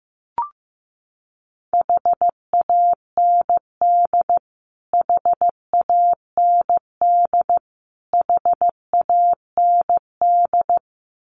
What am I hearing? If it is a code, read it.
Morse code, heard as a keyed tone, about 15 wpm, 700 Hz. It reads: HAND HAND HAND
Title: Top 200 Words Repeated 15wpm